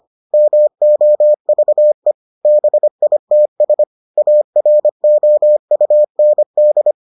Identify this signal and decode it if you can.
Morse code, heard as a keyed tone, about 25 wpm, 600 Hz. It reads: MOVE BITS AROUND